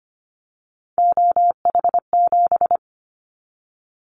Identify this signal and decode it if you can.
Morse code, heard as a keyed tone, about 25 wpm, 700 Hz. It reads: OH7
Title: Callsign Prefixes for Top 50 DXCC - 25wpm